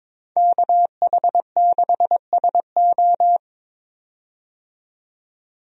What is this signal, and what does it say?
Morse code, heard as a keyed tone, about 22 wpm, 700 Hz. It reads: KH6SO
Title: DXCC-100 Call Signs - 5 Characters - 22wpm